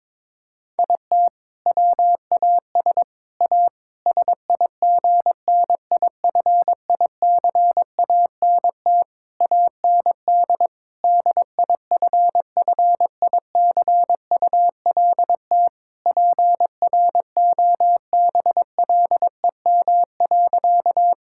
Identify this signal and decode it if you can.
Morse code, heard as a keyed tone, about 22 wpm, 700 Hz. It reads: IT WAS A SIGNIFICANT AND DIFFICULT PROBLEM.
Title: Sentences from Top 1000 Words - Review Part 6 - 22wpm